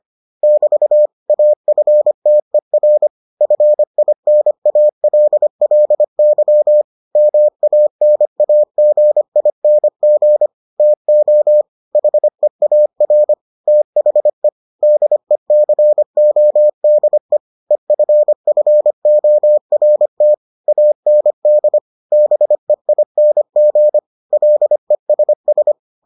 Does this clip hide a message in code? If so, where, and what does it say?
Yes: Morse code from 0.4 to 25.7 s, heard as a keyed tone, about 25 wpm, 600 Hz. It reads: <BT> AFTER FINALLY MANAGING TO HEAR THE DECODE EFFORT AND BEING LESS